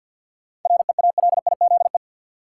Morse code, heard as a keyed tone, about 52 wpm, 700 Hz. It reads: REALIZE
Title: Sentences from Top 1000 Words - New Word Drill - 50wpm